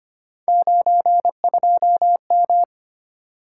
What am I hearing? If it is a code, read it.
Morse code, heard as a keyed tone, about 25 wpm, 700 Hz. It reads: 92M
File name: Three Letter-Number Combinations - 25wpm